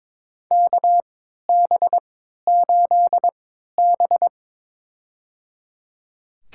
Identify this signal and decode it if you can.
Morse code, heard as a keyed tone, about 22 wpm, spaced as for 12 wpm, 700 Hz. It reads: KB8B
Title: USA Call Signs - 4 Characters - 3x Character Spacing - 22wpm